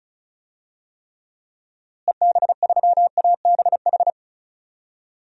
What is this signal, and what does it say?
Morse code, heard as a keyed tone, about 36 wpm, 700 Hz. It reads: EB3ABH
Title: DXCC-100 Call Signs - 6 Characters - Repeated 3x - 35wpm